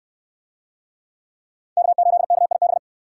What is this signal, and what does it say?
Morse code, heard as a keyed tone, about 68 wpm, 700 Hz. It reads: K8KIB